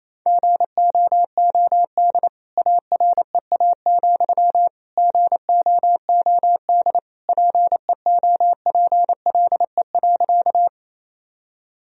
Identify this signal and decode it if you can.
Morse code, heard as a keyed tone, about 28 wpm, 700 Hz. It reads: GOOD AREA, GOOD PEOPLE.
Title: Sentences from Top 1000 Words - Review Part 2 - 28wpm